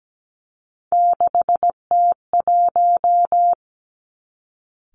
Morse code, heard as a keyed tone, about 17 wpm, 700 Hz. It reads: 6T1
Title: Three Challenging Letter-Number Combinations 17wpm